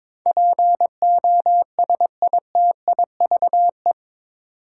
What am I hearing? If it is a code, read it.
Morse code, heard as a keyed tone, about 22 wpm, 700 Hz. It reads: POSITIVE
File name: practice / 22wpm / Eight Letter Words - 22wpm